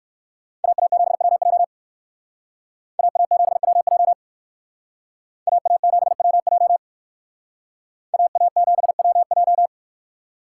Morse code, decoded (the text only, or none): AA7WJ AA7WJ AA7WJ AA7WJ